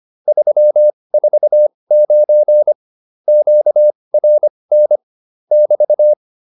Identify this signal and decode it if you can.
Morse code, heard as a keyed tone, about 25 wpm, spaced as for 20 wpm, 600 Hz. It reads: 349 QRN <BT>